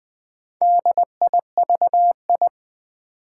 Morse code, heard as a keyed tone, about 20 wpm, 700 Hz. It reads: DIVI
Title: Build A Word - 2 Letters At A Time - 8-Letter Words - 20wpm